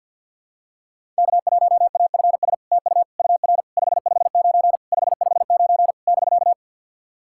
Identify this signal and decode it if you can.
Morse code, heard as a keyed tone, about 50 wpm, 700 Hz. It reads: K1AFS TU UR 559 559 <BK>
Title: SOTA QSO Practice 50wpm